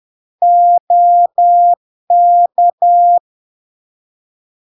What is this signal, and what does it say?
Morse code, heard as a keyed tone, about 10 wpm, 700 Hz. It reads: OK
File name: LARK-FD-2022-Log-in-CW-10wpm-700-Hz